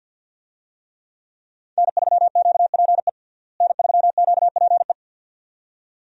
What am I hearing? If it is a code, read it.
Morse code, heard as a keyed tone, about 51 wpm, 700 Hz. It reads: N3QPE N3QPE
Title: USA Call Signs - 5 Characters - Repeated 2x - 50wpm